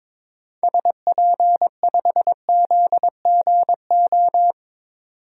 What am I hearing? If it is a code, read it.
Morse code, heard as a keyed tone, about 21 wpm, 700 Hz. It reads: SP5ZGO
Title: DXCC-100 Call Signs - 6 Characters - 22wpm